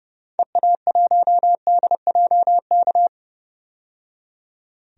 Morse code, heard as a keyed tone, about 30 wpm, 700 Hz. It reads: EA1DJK
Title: DXCC-100 Call Signs - 6 Characters - 30wpm